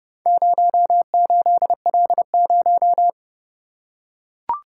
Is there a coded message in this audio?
Yes: Morse code from 0.3 to 3.1 s, heard as a keyed tone, about 29 wpm, 700 Hz. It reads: 08L0